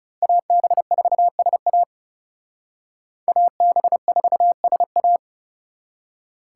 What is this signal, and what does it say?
Morse code, heard as a keyed tone, about 36 wpm, 700 Hz. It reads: AB4SA AB4SA
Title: Speed Racing - USA Call Signs 30wpm